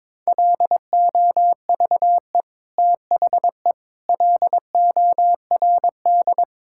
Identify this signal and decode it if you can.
Morse code, heard as a keyed tone, about 22 wpm, 700 Hz. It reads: LOVE THE LORD
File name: Sets of 9 Words - 22wpm